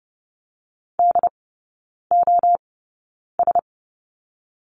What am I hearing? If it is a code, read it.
Morse code, heard as a keyed tone, about 30 wpm, 700 Hz. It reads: D O S
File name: Lesson 19e - Letter w - Three Characters 30wpm